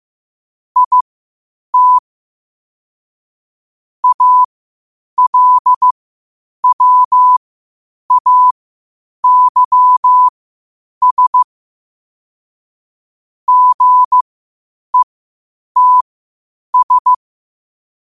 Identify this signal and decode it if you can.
Morse code, heard as a keyed tone, about 15 wpm, spaced as for 8 wpm, 1000 Hz. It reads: IT ALWAYS GETS
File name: qotdfiles / 8wpm